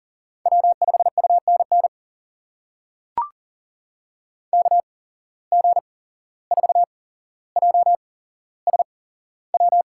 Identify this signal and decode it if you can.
Morse code, heard as a keyed tone, about 40 wpm, spaced as for 16 wpm, 700 Hz. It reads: W5UNN KG4JSW
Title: USA Call Signs - 8x Character Spacing - 40wpm